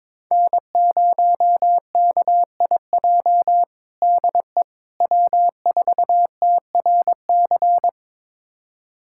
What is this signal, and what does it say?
Morse code, heard as a keyed tone, about 23 wpm, 700 Hz. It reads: N0KIJ DE W4TRC
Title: USA Callsign DE Callsign - 5 Characters - 22wpm